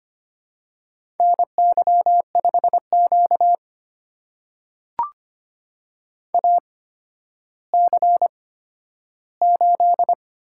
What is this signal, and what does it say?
Morse code, heard as a keyed tone, about 25 wpm, spaced as for 12 wpm, 700 Hz. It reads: NY5Q AC8
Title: USA Call Signs - 8x Character Spacing - 25wpm